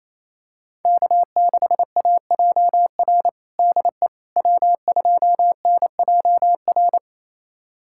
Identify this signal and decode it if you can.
Morse code, heard as a keyed tone, about 28 wpm, 700 Hz. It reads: K6AJR DE W2NJR
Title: USA Callsign DE Callsign - 5 Characters - 28wpm